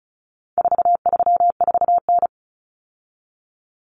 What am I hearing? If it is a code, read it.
Morse code, heard as a keyed tone, about 34 wpm, 700 Hz. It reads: 434N
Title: Common Cut Numbers - 4-Digit 35wpm